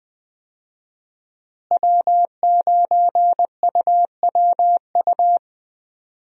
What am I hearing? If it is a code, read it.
Morse code, heard as a keyed tone, about 21 wpm, 700 Hz. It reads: W9UWU
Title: USA Call Signs - 5 Characters - Repeated 2x - 20wpm